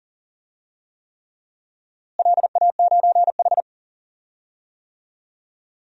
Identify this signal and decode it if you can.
Morse code, heard as a keyed tone, about 40 wpm, 700 Hz. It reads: LA9H
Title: DXCC-100 Call Signs - 4 Characters - Repeated 3x - 40wpm